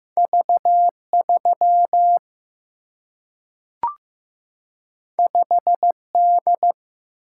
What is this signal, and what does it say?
Morse code, heard as a keyed tone, about 15 wpm, 700 Hz. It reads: V3 5D
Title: Single-Letter Single-Number 15wpm